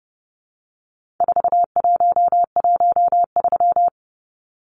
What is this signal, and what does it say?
Morse code, heard as a keyed tone, about 30 wpm, 700 Hz. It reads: 4113
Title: Numbers - 4-Digit 30wpm